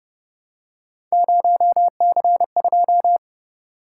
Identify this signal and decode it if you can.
Morse code, heard as a keyed tone, about 30 wpm, 700 Hz. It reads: 0C2
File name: Speed-Racing - Three Letter-Number Combinations - 25wpm